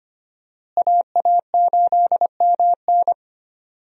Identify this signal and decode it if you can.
Morse code, heard as a keyed tone, about 25 wpm, 700 Hz. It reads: AA8MN
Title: USA Call Signs - 5 Characters - Repeated 3x - 25wpm